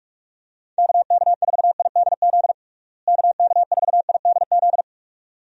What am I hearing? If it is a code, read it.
Morse code, heard as a keyed tone, about 41 wpm, 700 Hz. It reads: KK4IDZ KK4IDZ
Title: USA Call Signs - 6 Characters - Repeated 2x - 45wpm